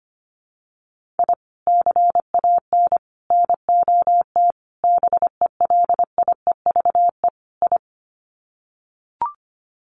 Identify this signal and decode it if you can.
Morse code, heard as a keyed tone, about 25 wpm, 700 Hz. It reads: I CAN NOT BELIEVE I
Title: Sets of 5 Words Encore2 25wpm